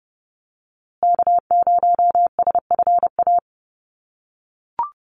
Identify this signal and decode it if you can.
Morse code, heard as a keyed tone, about 30 wpm, 700 Hz. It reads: K0SFA
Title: USA Call Signs Repeated 2x 30wpm